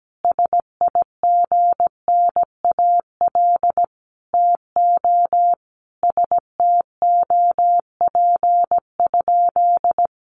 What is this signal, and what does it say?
Morse code, heard as a keyed tone, about 17 wpm, 700 Hz. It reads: SIGNAL TO STOP?
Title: Sentences from Top 1400 Words 17wpm